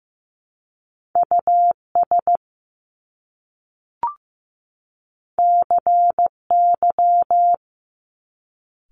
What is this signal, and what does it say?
Morse code, heard as a keyed tone, about 15 wpm, 700 Hz. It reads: US CY
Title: Top Two Letter Combinations in Words 15wpm